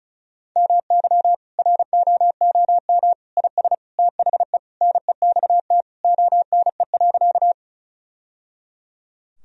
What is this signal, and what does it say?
Morse code, heard as a keyed tone, about 36 wpm, 700 Hz. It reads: MY ROOM IS THE NEXT ONE.
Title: Sentences from Top 1000 Words - Review Part 1 - 35wpm